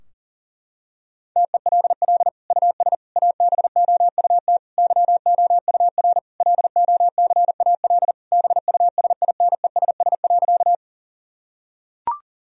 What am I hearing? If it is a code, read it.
Morse code, heard as a keyed tone, about 40 wpm, 700 Hz. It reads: TELL US ABOUT YOUR LOCAL BUSINESS.
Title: Sentences from Top 1000 Words - Review Part 3 - 40wpm